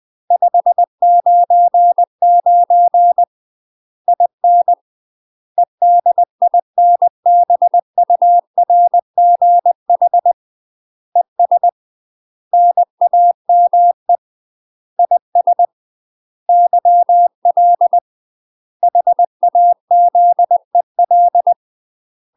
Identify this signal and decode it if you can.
Morse code, heard as a keyed tone, about 20 wpm, 700 Hz. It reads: 599 IN EDINBURGH ES NAME IS YL HAZEL
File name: UK MASTER QSOs TRAINING / 20WPM / UK MASTER QSOs 0036